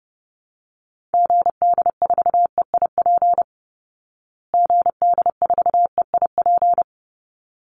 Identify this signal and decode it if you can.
Morse code, heard as a keyed tone, about 30 wpm, 700 Hz. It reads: GD4EIP GD4EIP
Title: Call Signs Repeated 3x - 30wpm